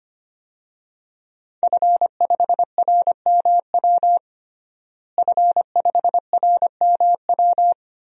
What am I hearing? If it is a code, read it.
Morse code, heard as a keyed tone, about 25 wpm, 700 Hz. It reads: F5RMW F5RMW
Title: DXCC-100 Call Signs - 5 Characters - Repeated 3x - 25wpm